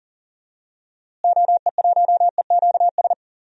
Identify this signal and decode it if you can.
Morse code, heard as a keyed tone, about 40 wpm, 700 Hz. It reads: OE1EQS
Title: DXCC-100 Call Signs - 6 Characters - Repeated 3x - 40wpm